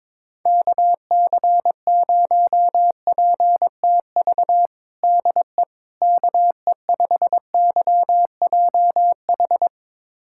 Morse code, heard as a keyed tone, about 22 wpm, 700 Hz. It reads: KC0PTV DE KE5YJH
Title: USA Callsign DE Callsign 22wpm